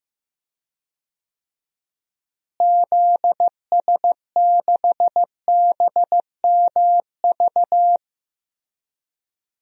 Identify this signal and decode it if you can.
Morse code, heard as a keyed tone, about 16 wpm, 700 Hz. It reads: ZS6BMV
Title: DXCC-100 Call Signs - 6 Characters - Repeated 3x - 15wpm